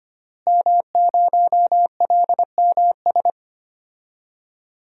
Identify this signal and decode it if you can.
Morse code, heard as a keyed tone, about 25 wpm, 700 Hz. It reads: M0LMS
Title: DXCC-100 Call Signs - 5 Characters - Repeated 2x - 25wpm